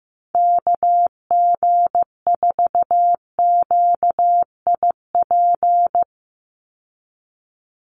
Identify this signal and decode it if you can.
Morse code, heard as a keyed tone, about 15 wpm, 700 Hz. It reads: KG4QIP
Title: USA Call Signs 15wpm